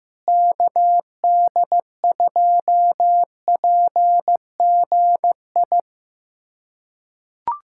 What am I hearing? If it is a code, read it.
Morse code, heard as a keyed tone, about 15 wpm, 700 Hz. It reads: KD2PGI
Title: USA Callsign DE Callsign - 6 Characters - 15wpm